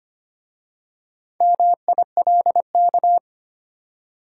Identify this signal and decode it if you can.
Morse code, heard as a keyed tone, about 25 wpm, 700 Hz. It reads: MILK
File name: practice / 25wpm / Easy Words with Hints - 25wpm